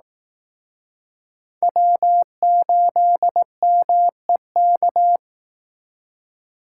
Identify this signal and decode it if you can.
Morse code, heard as a keyed tone, about 18 wpm, 700 Hz. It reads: W8MEK
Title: Speed Racing - USA 1x3 Call Signs 15wpm